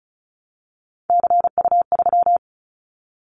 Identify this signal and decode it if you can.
Morse code, heard as a keyed tone, about 36 wpm, 700 Hz. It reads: CU3
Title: Callsign Prefixes for Top 100 DXCC - 35wpm